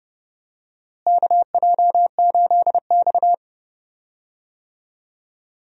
Morse code, heard as a keyed tone, about 30 wpm, 700 Hz. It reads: KJ8X
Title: USA Call Signs - 4 Characters - Repeated 3x - 30wpm